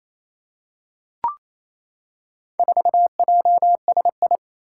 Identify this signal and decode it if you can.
Morse code, heard as a keyed tone, about 28 wpm, 700 Hz. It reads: 4JSI
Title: Four Challenging Letter-Number Combinations 28wpm